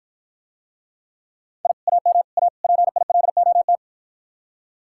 Morse code, heard as a keyed tone, about 54 wpm, 700 Hz. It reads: I AM A PILOT